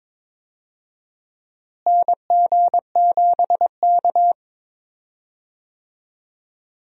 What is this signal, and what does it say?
Morse code, heard as a keyed tone, about 22 wpm, 700 Hz. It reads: NG7K
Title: USA Call Signs - 4 Characters - Repeated 2x - 22wpm